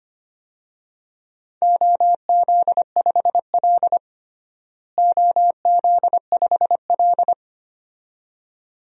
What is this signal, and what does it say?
Morse code, heard as a keyed tone, about 25 wpm, 700 Hz. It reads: OZ5L OZ5L
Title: DXCC-100 Call Signs - 4 Characters - Repeated 2x - 25wpm